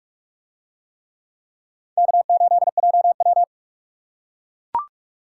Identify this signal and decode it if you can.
Morse code, heard as a keyed tone, about 39 wpm, 700 Hz. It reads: K8JW